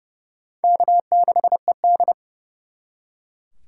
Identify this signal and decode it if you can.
Morse code, heard as a keyed tone, about 38 wpm, 700 Hz. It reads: K6ED